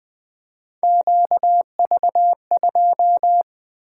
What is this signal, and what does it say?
Morse code, heard as a keyed tone, about 20 wpm, 700 Hz. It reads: QV2
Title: Three Letter-Number Combinations - 20wpm